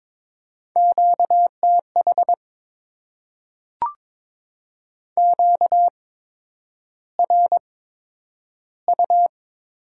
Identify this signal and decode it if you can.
Morse code, heard as a keyed tone, about 22 wpm, spaced as for 9 wpm, 700 Hz. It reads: QTH QRU
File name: Top 90 QSO Elements - 8x Character Spacing - 22wpm